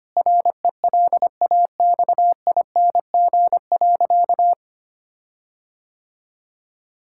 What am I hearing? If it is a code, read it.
Morse code, heard as a keyed tone, about 25 wpm, 700 Hz. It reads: RELAXING.
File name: practice / 25wpm / Difficult Sentences with Hints - 25wpm